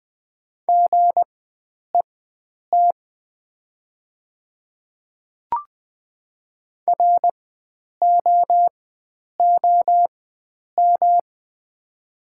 Morse code, heard as a keyed tone, about 20 wpm, spaced as for 9 wpm, 690 Hz. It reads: GET ROOM
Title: 4x Character Spacing - Top 300 Words - 20wpm